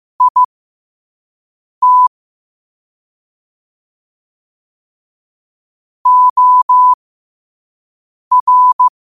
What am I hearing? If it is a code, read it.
Morse code, heard as a keyed tone, about 15 wpm, spaced as for 5 wpm, 1000 Hz. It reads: IT OR
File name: qotdfiles / 5wpm